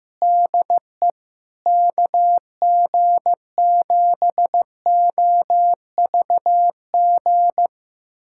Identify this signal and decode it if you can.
Morse code, heard as a keyed tone, about 15 wpm, 700 Hz. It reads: DE KG7OVG
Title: USA Callsign DE Callsign - 6 Characters - 15wpm